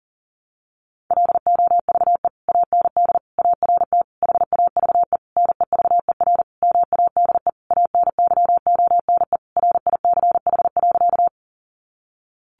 Morse code, heard as a keyed tone, about 40 wpm, 700 Hz. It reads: LOVE AND ART HAVE NEVER MADE ANYONE RICH.
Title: Sentences from Top 1100 Words 40wpm